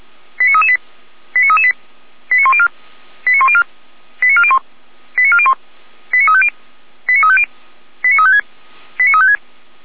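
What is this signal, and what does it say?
Five-tone selective call (ZVEI), heard as quick runs of five tones, tones of about 70 ms. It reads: 89338 89338 89104 89104 89401 89401 89350 89350 89356 89356
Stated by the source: Received on an SDR; ZVEI five-tone code 89080.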